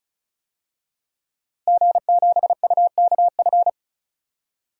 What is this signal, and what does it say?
Morse code, heard as a keyed tone, about 34 wpm, 700 Hz. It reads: G7UKF
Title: DXCC-100 Call Signs - 5 Characters - Repeated 2x - 35wpm